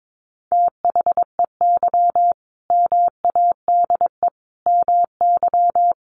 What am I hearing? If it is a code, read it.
Morse code, heard as a keyed tone, about 22 wpm, 700 Hz. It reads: THEY MADE MY
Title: Sentences from Top 1500 Words 22wpm